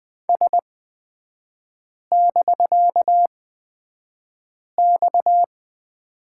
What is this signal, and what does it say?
Morse code, heard as a keyed tone, about 20 wpm, spaced as for 5 wpm, 700 Hz. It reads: S<BK>X
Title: 250 Lesson 40e - BK - Three Characters 20wpm